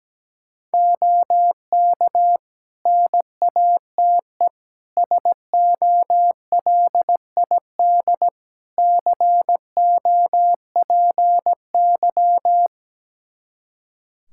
Morse code, heard as a keyed tone, about 17 wpm, 700 Hz. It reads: OK NATE SOLID COPY
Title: Practice QSO - Exchange 3 - More Info - 17wpm